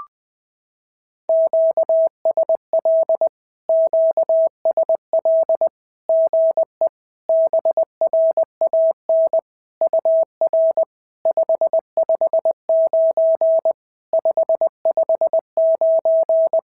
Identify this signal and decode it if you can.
Morse code, heard as a keyed tone, about 20 wpm, 650 Hz. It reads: QSL QSL GE BRAN UR 559 559
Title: SOTA QSO Practice 20wpm